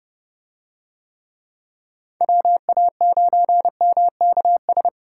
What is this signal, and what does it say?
Morse code, heard as a keyed tone, about 30 wpm, 700 Hz. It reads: WA9MKS